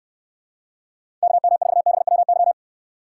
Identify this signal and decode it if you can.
Morse code, heard as a keyed tone, about 64 wpm, 700 Hz. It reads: KM4BWY